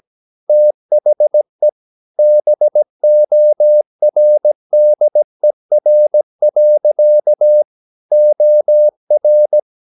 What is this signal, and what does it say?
Morse code, heard as a keyed tone, about 17 wpm, 600 Hz. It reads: THE BORDER. OR
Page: Romanen Never af Ken Follet på engelsk.